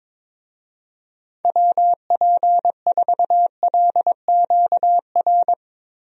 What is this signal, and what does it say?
Morse code, heard as a keyed tone, about 22 wpm, 700 Hz. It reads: WP4LQR
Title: USA Call Signs - 6 Characters - Repeated 3x - 22wpm